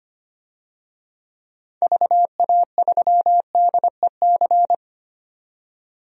Morse code, heard as a keyed tone, about 25 wpm, 700 Hz. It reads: VA3DEC
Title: DXCC-100 Call Signs - 6 Characters - 25wpm